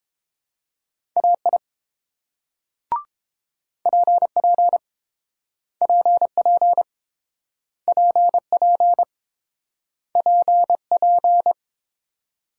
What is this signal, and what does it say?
Morse code, heard as a keyed tone, about 30 wpm, 700 Hz. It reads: AI PP PP PP PP